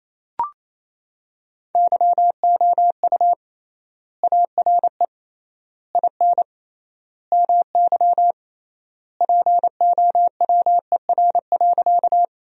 Morse code, heard as a keyed tone, about 27 wpm, 700 Hz. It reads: YOU ARE IN MY POWER.
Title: 3x Word-Spacing - Sentences from Top 500 Words - 28wpm